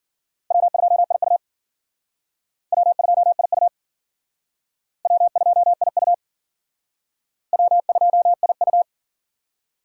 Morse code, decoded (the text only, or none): W2IU W2IU W2IU W2IU